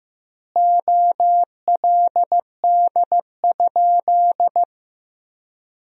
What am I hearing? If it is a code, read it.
Morse code, heard as a keyed tone, about 15 wpm, 700 Hz. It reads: OLD?
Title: 2x Word-Spacing - Sentences from Top 400 Words - 15wpm